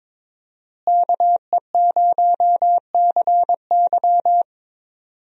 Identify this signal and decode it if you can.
Morse code, heard as a keyed tone, about 22 wpm, 700 Hz. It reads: KE0CY
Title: USA Call Signs - 5 Characters - Repeated 2x - 22wpm